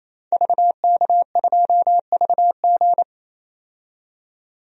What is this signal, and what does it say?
Morse code, heard as a keyed tone, about 28 wpm, 700 Hz. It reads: VK2VG
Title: DXCC-100 Call Signs - 5 Characters - 28wpm